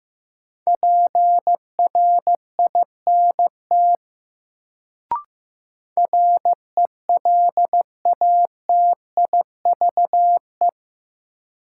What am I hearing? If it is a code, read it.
Morse code, heard as a keyed tone, about 16 wpm, 700 Hz. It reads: PRINT RELATIVE
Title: Sentences from Top 1400 Words - New Word Drill - 15wpm